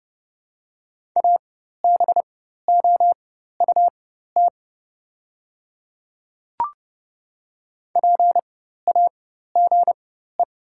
Morse code, heard as a keyed tone, about 30 wpm, spaced as for 14 wpm, 700 Hz. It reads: ABOUT PAGE
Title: 4x Character Spacing - Top 300 Words - 30wpm